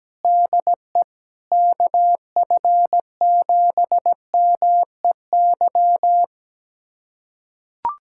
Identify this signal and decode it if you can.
Morse code, heard as a keyed tone, about 17 wpm, 700 Hz. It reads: DE KF7MEY
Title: USA Callsign DE Callsign - 6 Characters - 17wpm